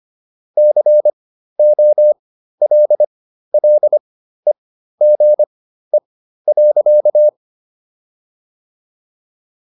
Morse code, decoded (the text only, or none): COLLEGE.